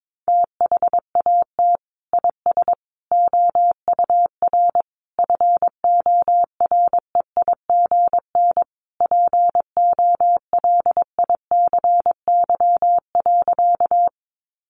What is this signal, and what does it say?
Morse code, heard as a keyed tone, about 22 wpm, 700 Hz. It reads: THAT IS OUR FOREIGN POLICY.
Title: Sentences from Top 1400 Words 22wpm